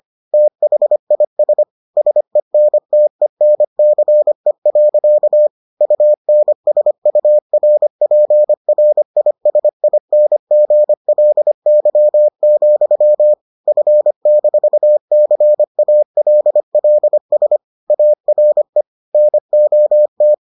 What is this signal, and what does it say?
Morse code, heard as a keyed tone, about 25 wpm, 600 Hz. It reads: THIS SENTENCE. UNSURPRISINGLY, F-CALLS ARE NOT